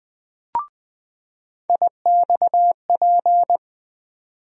Morse code, heard as a keyed tone, about 20 wpm, 700 Hz. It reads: IXP
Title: USA Call Sign Postfixes - 3 Letters 20wpm